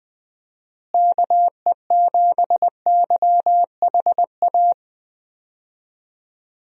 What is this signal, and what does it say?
Morse code, heard as a keyed tone, about 20 wpm, 700 Hz. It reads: KE7YHA
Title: USA Call Signs - 6 Characters - Repeated 3x - 20wpm